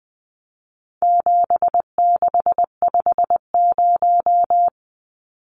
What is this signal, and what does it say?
Morse code, heard as a keyed tone, about 20 wpm, 700 Hz. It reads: 7650
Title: Numbers - 4-Digit 20wpm